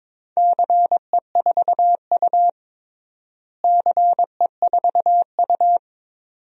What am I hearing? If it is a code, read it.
Morse code, heard as a keyed tone, about 22 wpm, 700 Hz. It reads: CE4U CE4U
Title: DXCC-100 Call Signs - 4 Characters - Repeated 2x - 22wpm